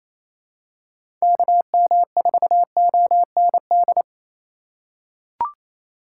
Morse code, heard as a keyed tone, about 28 wpm, 700 Hz. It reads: KM4OND
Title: USA Call Signs - 6 Characters - 4x Character Spacing - 28wpm